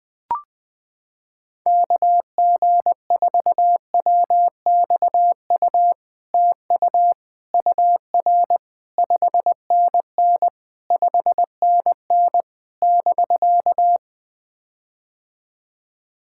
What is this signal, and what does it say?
Morse code, heard as a keyed tone, about 20 wpm, 700 Hz. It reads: KG4WXU TU UR 5NN 5NN <BK>
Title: POTA QSO Practice - 20wpm